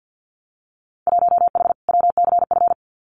Morse code, heard as a keyed tone, about 50 wpm, 700 Hz. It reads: 1H WCF
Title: Winter Field Day Exchanges 50wpm